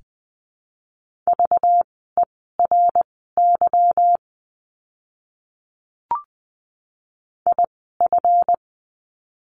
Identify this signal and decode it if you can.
Morse code, heard as a keyed tone, about 20 wpm, spaced as for 14 wpm, 700 Hz. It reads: VERY IF
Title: 237 Lesson 38f - Letter q - Words 20wpm